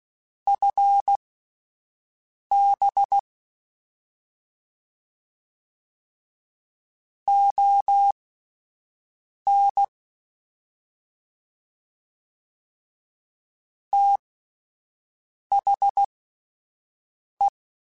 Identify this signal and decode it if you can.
Morse code, heard as a keyed tone, about 16 wpm, spaced as for 5 wpm, 800 Hz. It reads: FB ON THE
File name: (spaced) / QSOs Spaced-mp3s / WD0FBT 1 @10 sp